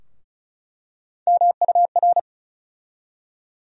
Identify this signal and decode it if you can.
Morse code, heard as a keyed tone, about 36 wpm, 700 Hz. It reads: MUR
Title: USA Call Sign Postfixes - 3 Letters 35wpm